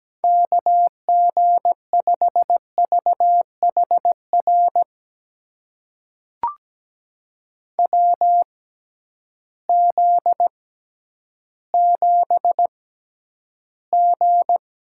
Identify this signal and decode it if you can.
Morse code, heard as a keyed tone, about 17 wpm, spaced as for 10 wpm, 700 Hz. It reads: KG5VHR WZ7G
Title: USA Call Signs - 6x Character Spacing - 17wpm